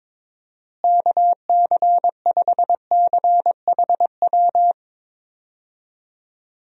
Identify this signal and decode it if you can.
Morse code, heard as a keyed tone, about 21 wpm, 700 Hz. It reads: KC5CHW
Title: USA Call Signs - 6 Characters - Repeated 3x - 22wpm